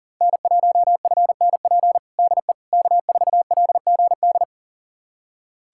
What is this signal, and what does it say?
Morse code, heard as a keyed tone, about 40 wpm, 700 Hz. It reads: N1FNP DE K4LGD
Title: USA Callsign DE Callsign - 5 Characters - 40wpm